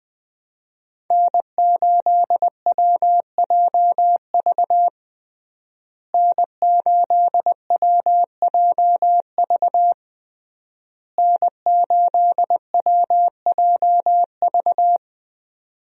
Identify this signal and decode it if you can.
Morse code, heard as a keyed tone, about 20 wpm, 700 Hz. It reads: N8WJV N8WJV N8WJV
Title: USA Call Signs - 5 Characters - Repeated 3x - 20wpm